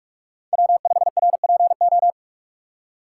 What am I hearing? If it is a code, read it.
Morse code, heard as a keyed tone, about 46 wpm, 700 Hz. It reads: W5RPO
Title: USA Call Signs - 5 Characters - 4x Character Spacing - 45wpm